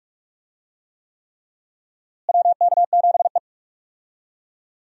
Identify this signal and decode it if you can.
Morse code, heard as a keyed tone, about 45 wpm, 700 Hz. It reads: WK7E
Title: USA Call Signs - 4 Characters - Repeated 3x - 45wpm